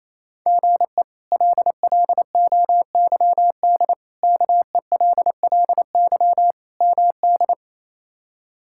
Morse code, heard as a keyed tone, about 27 wpm, 700 Hz. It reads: GE LLOYD KELLY MD
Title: K1USN SST QSO Practice - 28wpm